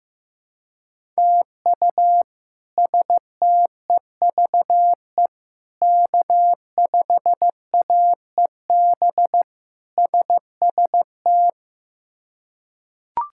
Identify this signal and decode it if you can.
Morse code, heard as a keyed tone, about 15 wpm, 700 Hz. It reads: TU STEVE K5AEB SST
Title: K1USN SST QSO Practice - 15wpm